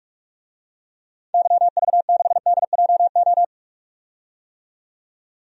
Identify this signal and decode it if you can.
Morse code, heard as a keyed tone, about 41 wpm, 700 Hz. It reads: YV6DJO
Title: DXCC-100 Call Signs - 6 Characters - 45wpm